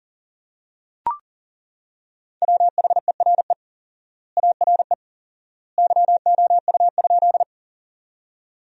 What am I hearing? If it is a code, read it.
Morse code, heard as a keyed tone, about 40 wpm, 700 Hz. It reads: WHERE ARE YOU?
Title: 4x Word-Spacing - Sentences from Top 200 Words - 40wpm